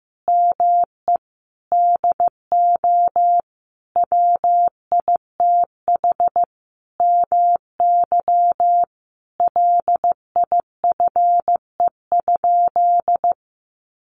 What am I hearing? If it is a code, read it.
Morse code, heard as a keyed tone, about 15 wpm, 700 Hz. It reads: ME DO WITH MY LIFE?
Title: Sentences from Top 300 Words 15wpm